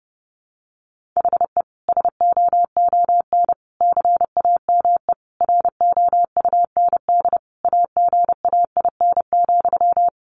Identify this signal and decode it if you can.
Morse code, heard as a keyed tone, about 30 wpm, 700 Hz. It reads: HE SOON CAME ROUND AGAIN,
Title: Sentences from Top 600 Words 30wpm